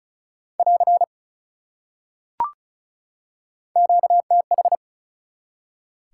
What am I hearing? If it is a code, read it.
Morse code, heard as a keyed tone, about 34 wpm, 700 Hz. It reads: <AR> QTH
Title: Top 180 QSO Elements 35wpm